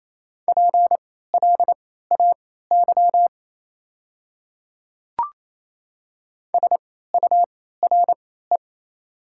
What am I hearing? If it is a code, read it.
Morse code, heard as a keyed tone, about 28 wpm, spaced as for 16 wpm, 700 Hz. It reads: PLAY SURE
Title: 3x Character Spacing - Top 300 Words - 28wpm